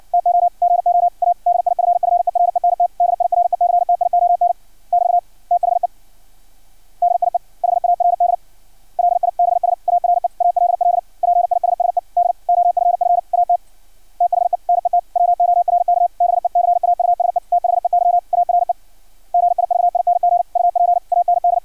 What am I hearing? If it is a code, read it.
Morse code, heard as a keyed tone, about 56 wpm, 700 Hz. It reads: TO GO A DIFFERENT DIRECTION <BT> THE DIE HARD LIBS ARE ALL PISSED OFF AT THE NETWORK BECAUSE THEY ARE GIVING UP AND
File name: archived / CRSnet